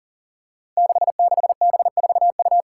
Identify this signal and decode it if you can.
Morse code, heard as a keyed tone, about 40 wpm, 700 Hz. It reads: 66B4U
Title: Five Challenging Letter-Number Combinations 40wpm